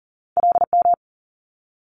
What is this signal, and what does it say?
Morse code, heard as a keyed tone, about 41 wpm, 700 Hz. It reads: LM
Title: Multi-letter Phonemes Quick Fire 40wpm